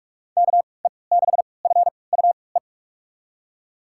Morse code, heard as a keyed tone, about 44 wpm, spaced as for 25 wpm, 700 Hz. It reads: KE6FUE
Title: USA Call Signs - 3x Character Spacing - 45wpm